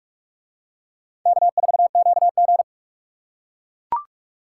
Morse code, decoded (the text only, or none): K4QG